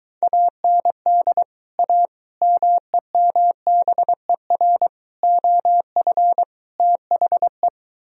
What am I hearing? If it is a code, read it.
Morse code, heard as a keyed tone, about 22 wpm, 700 Hz. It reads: AND A MEMBER OF THE